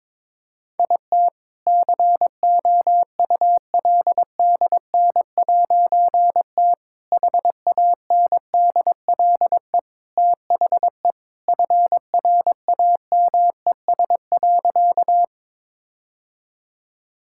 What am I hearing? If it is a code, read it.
Morse code, heard as a keyed tone, about 22 wpm, 700 Hz. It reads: IT COULDN'T HANDLE THE FRAMES.
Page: morse-code-ninja-practice-22wpm